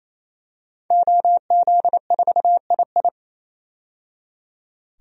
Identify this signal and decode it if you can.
Morse code, heard as a keyed tone, about 28 wpm, 700 Hz. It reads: OZ4II
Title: DXCC-100 Call Signs - 5 Characters - 28wpm